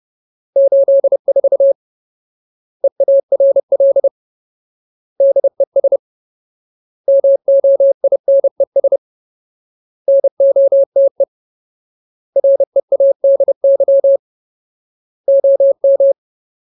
Morse code, heard as a keyed tone, about 30 wpm, 550 Hz. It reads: 84 EARL DES MOINES NOTE READY OM
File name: Ultimate Word List / 30WPM Ultimate